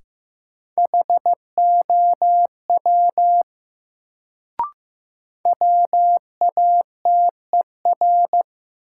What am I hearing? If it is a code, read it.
Morse code, heard as a keyed tone, about 15 wpm, 700 Hz. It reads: HOW WATER
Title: Sentences from Top 100 Words - New Word Drill - 15wpm